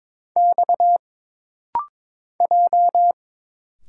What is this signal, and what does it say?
Morse code, heard as a keyed tone, about 22 wpm, 700 Hz. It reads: X J
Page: This rapid-fire timing is geared to push your ICR (Instant Character Recognition) skills.